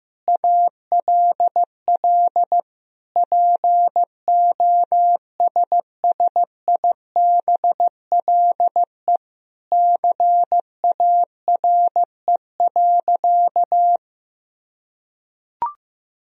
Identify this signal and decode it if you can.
Morse code, heard as a keyed tone, about 15 wpm, 700 Hz. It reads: ALL POSSIBLE CARE.
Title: Sentences from Top 1000 Words - Review Part 2 - 15wpm